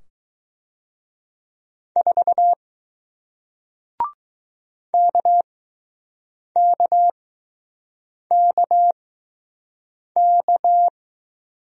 Morse code, heard as a keyed tone, about 20 wpm, spaced as for 6 wpm, 700 Hz. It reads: 4 KKKK